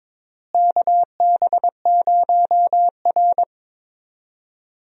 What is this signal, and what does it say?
Morse code, heard as a keyed tone, about 21 wpm, 700 Hz. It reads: KB0R
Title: USA Call Signs - 4 Characters - Repeated 3x - 22wpm